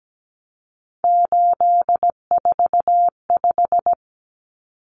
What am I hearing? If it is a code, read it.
Morse code, heard as a keyed tone, about 17 wpm, 700 Hz. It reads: 845
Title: Common Cut Numbers - 3-Digit 17wpm